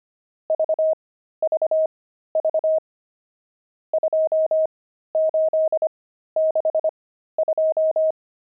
Morse code, heard as a keyed tone, about 25 wpm, spaced as for 13 wpm, 630 Hz. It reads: VVV 2862
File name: pangram-fs-13-cs-25-nums16